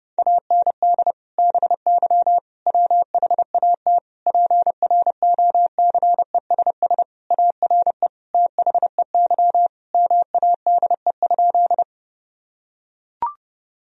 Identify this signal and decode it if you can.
Morse code, heard as a keyed tone, about 30 wpm, 700 Hz. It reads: AND BY WHAT PROCESS ARE THEY MADE?
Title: Sentences from Top 1000 Words - Review Part 3 - 30wpm